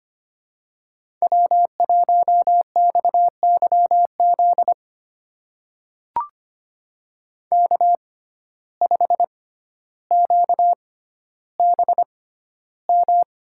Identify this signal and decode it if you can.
Morse code, heard as a keyed tone, about 25 wpm, spaced as for 13 wpm, 690 Hz. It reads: W1XYZ K5QBM
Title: USA Call Signs - 5 Characters - 6x Character Spacing - 25wpm